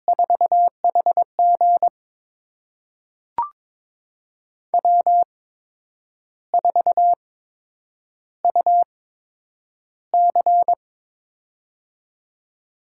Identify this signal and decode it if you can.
Morse code, heard as a keyed tone, about 23 wpm, spaced as for 9 wpm, 700 Hz. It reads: 4HG W4UC
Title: USA Call Signs - 4 Characters - 8x Character Spacing - 22wpm